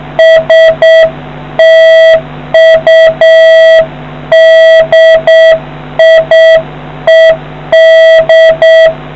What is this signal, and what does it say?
Morse code, heard as a keyed tone, about 7 wpm, 650 Hz. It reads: STUDIED